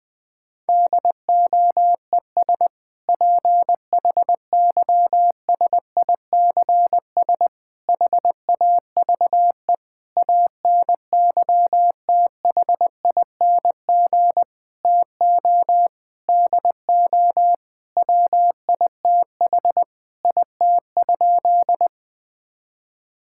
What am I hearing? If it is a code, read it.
Morse code, heard as a keyed tone, about 20 wpm, 700 Hz. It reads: DOES PHYSICS HAVE ANYTHING TO DO WITH IT?
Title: Sentences from Top 1000 Words - Review Part 5 - 20wpm